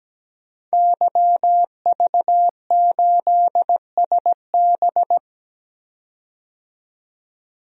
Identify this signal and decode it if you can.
Morse code, heard as a keyed tone, about 17 wpm, 700 Hz. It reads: YV8SB
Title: DXCC-100 Call Signs - 5 Characters - 17wpm